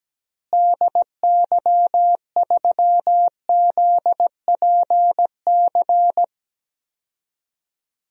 Morse code, heard as a keyed tone, about 17 wpm, 700 Hz. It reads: DY3ZPC
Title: DXCC-100 Call Signs - 6 Characters - Repeated 3x - 17wpm